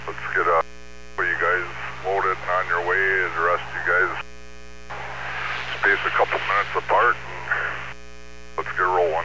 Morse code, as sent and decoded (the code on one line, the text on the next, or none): none